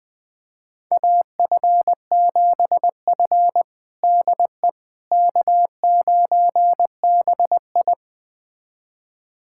Morse code, heard as a keyed tone, about 20 wpm, 700 Hz. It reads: AF7F DE K9BI
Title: USA Callsign DE Callsign - 4 Characters - 20wpm